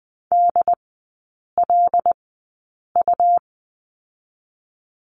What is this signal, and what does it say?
Morse code, heard as a keyed tone, about 20 wpm, spaced as for 8 wpm, 700 Hz. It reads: DLU
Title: 107 Lesson 18e - Letter m - Three Characters 20wpm